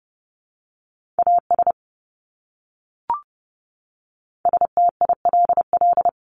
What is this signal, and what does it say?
Morse code, heard as a keyed tone, about 30 wpm, 700 Hz. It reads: AS STILL
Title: Top 300 Words Repeated 30wpm